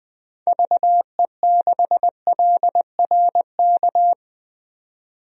Morse code, heard as a keyed tone, about 21 wpm, 700 Hz. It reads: VE6LRK
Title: DXCC-100 Call Signs - 6 Characters - 20wpm